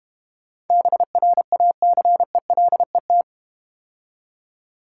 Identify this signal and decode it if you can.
Morse code, heard as a keyed tone, about 31 wpm, 700 Hz. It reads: BRACELET
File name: Easy Words - Speed-Racing with Hints - 25wpm